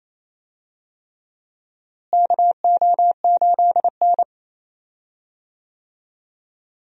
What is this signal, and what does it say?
Morse code, heard as a keyed tone, about 27 wpm, 700 Hz. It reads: KO8N
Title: USA Call Signs - 4 Characters - Repeated 3x - 28wpm